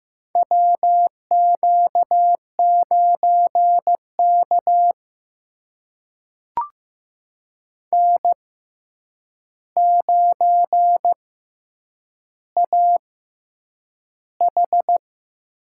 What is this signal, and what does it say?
Morse code, heard as a keyed tone, about 15 wpm, spaced as for 8 wpm, 700 Hz. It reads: WQ9K N9AH
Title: USA Call Signs - 4 Characters - 6x Character Spacing - 15wpm